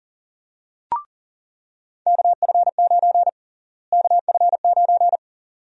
Morse code, heard as a keyed tone, about 40 wpm, 700 Hz. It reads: KF9 KF9
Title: USA Call Sign Prefixes - Repeated 2x - 40wpm